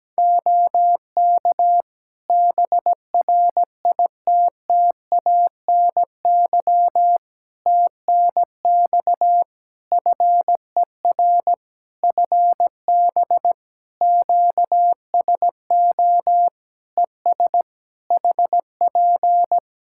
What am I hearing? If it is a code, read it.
Morse code, heard as a keyed tone, about 17 wpm, 700 Hz. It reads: OK BRITTANY TNX FER FB QSO ES HP
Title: Practice QSO - Exchange 4 - Ending - 17wpm